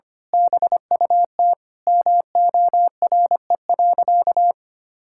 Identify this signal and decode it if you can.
Morse code, heard as a keyed tone, about 25 wpm, 700 Hz. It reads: BUT MORE.
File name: practice / 25wpm / Sentences with 2-Word Phrases - 25wpm